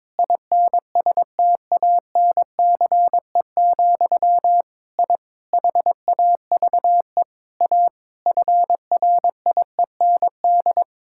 Morse code, heard as a keyed tone, about 23 wpm, 700 Hz. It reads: INSTANCE, I HAVE A FRIEND
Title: Sentences from Top 1000 Words - Review Part 5 - 22wpm